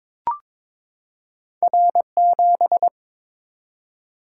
Morse code, heard as a keyed tone, about 23 wpm, 700 Hz. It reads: R7
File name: practice / 22wpm / Two Letter-Number Combinations - 22wpm